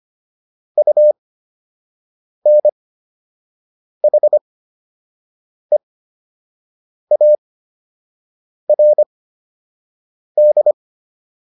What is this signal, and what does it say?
Morse code, heard as a keyed tone, about 25 wpm, spaced as for 6 wpm, 600 Hz. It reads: UNHEARD
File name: UN7 06